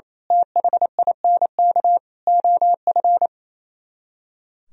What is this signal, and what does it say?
Morse code, heard as a keyed tone, about 27 wpm, 700 Hz. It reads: THINK OF
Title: Speed-Racing - Sets of 5 Words 28wpm